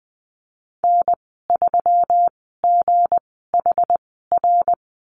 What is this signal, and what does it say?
Morse code, heard as a keyed tone, about 20 wpm, spaced as for 15 wpm, 700 Hz. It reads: N3GHR
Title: 186 Lesson 30g - Letter b - Call Signs 20wpm